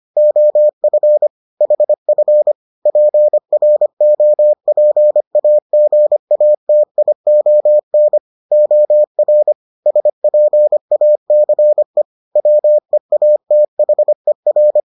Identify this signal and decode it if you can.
Morse code, heard as a keyed tone, about 25 wpm, 600 Hz. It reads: OF HF PROPAGATION OR SPACE WEATHER